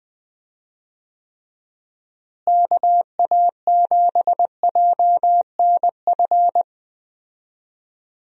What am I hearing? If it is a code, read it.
Morse code, heard as a keyed tone, about 20 wpm, 700 Hz. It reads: KA7JNF
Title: USA Call Signs - 6 Characters - Repeated 2x - 20wpm